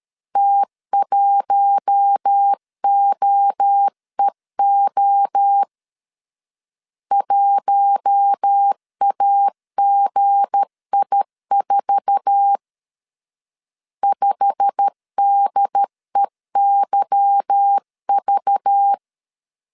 Morse code, heard as a keyed tone, about 13 wpm, 800 Hz. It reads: T1OEO 1AGI4 5DEYV
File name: LEZIONI / 8 YJ45 / 8 YJ45 12wpm 2151